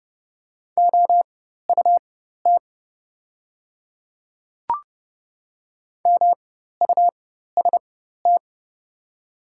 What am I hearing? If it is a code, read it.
Morse code, heard as a keyed tone, about 30 wpm, spaced as for 14 wpm, 700 Hz. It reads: OUT MUST
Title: 4x Character Spacing - Top 200 Words - 30wpm